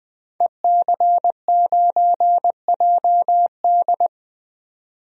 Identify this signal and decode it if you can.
Morse code, heard as a keyed tone, about 20 wpm, 700 Hz. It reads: EC9JD
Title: DXCC-100 Call Signs - 5 Characters - 20wpm